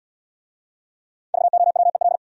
Morse code, heard as a keyed tone, about 71 wpm, 700 Hz. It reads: VOWEL